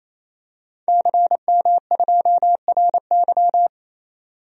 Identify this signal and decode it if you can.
Morse code, heard as a keyed tone, about 28 wpm, 700 Hz. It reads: CM2RY
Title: DXCC-100 Call Signs - 5 Characters - 28wpm